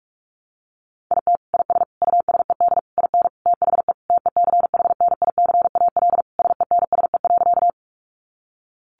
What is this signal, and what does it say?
Morse code, heard as a keyed tone, about 44 wpm, 700 Hz. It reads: IT IS USED IN THE TECHNICAL SENSE.
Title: Sentences from Top 1100 Words 45wpm